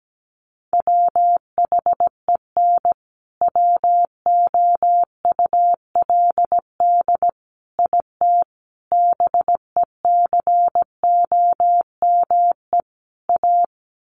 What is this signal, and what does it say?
Morse code, heard as a keyed tone, about 17 wpm, 700 Hz. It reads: WHEN WOULD IT BECOME A
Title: Sentences from Top 1200 Words 17wpm